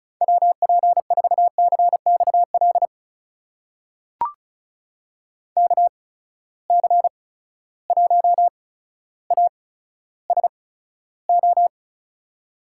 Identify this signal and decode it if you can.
Morse code, heard as a keyed tone, about 36 wpm, spaced as for 15 wpm, 700 Hz. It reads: WP4CXL KC1ASO
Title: USA Call Signs - 6 Characters - 8x Character Spacing - 35wpm